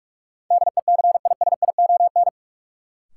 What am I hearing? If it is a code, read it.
Morse code, heard as a keyed tone, about 43 wpm, 700 Hz. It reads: DECISION
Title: Sentences from Top 1000 Words - New Word Drill - 45wpm